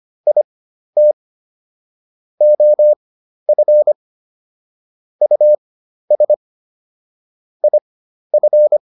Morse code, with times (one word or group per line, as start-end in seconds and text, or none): 0.3-1.1 IT
2.4-3.9 OF
5.2-6.3 US
7.6-8.8 IF